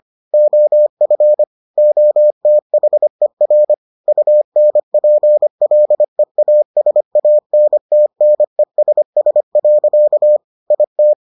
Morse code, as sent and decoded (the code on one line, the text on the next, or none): --- ..-. / --- - .... .-. / ..- -. .--. .-.. .- ... .- -. - -. ... ... .-.-.- / .. -
OF OTHER UNPLEASANTNESS. IT